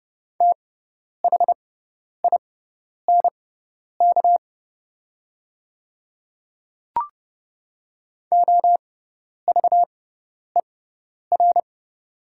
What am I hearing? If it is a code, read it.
Morse code, heard as a keyed tone, about 29 wpm, spaced as for 10 wpm, 700 Hz. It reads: THINK OVER
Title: 6x Character Spacing - Top 200 Words - 30wpm